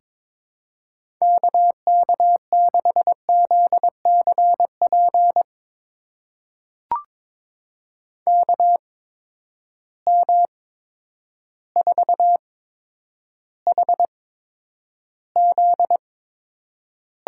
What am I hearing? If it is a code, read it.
Morse code, heard as a keyed tone, about 23 wpm, spaced as for 10 wpm, 700 Hz. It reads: KK6ZCP KM4HZ
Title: USA Call Signs - 6 Characters - 8x Character Spacing - 22wpm